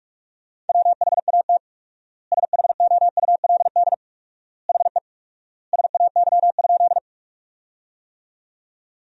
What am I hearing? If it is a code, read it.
Morse code, heard as a keyed tone, about 42 wpm, 700 Hz. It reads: WHAT SHOULD HE SAY?
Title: 4x Word-Spacing - Sentences from Top 300 Words - 45wpm